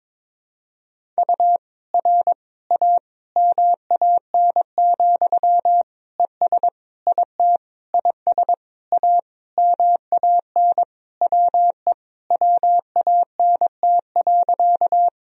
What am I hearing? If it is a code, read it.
Morse code, heard as a keyed tone, about 22 wpm, 700 Hz. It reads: U R A MAN, ES IT IS A MAN WE WANT.
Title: Sentences from Top 200 Words and Abbreviations 22wpm